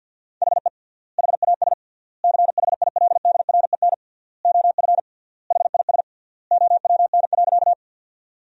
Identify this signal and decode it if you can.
Morse code, heard as a keyed tone, about 51 wpm, 700 Hz. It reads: HE HAS CHILDREN OF HIS OWN.
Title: 3x Word-Spacing - Sentences from Top 300 Words - 50wpm